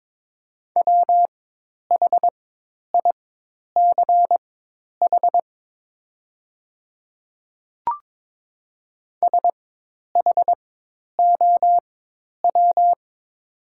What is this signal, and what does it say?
Morse code, heard as a keyed tone, about 22 wpm, spaced as for 10 wpm, 700 Hz. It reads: WHICH SHOW
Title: 4x Character Spacing - Top 200 Words - 22wpm